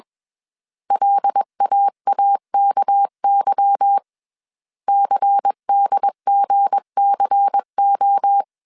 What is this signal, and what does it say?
Morse code, heard as a keyed tone, about 21 wpm, 800 Hz. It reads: LAAKY CDGCO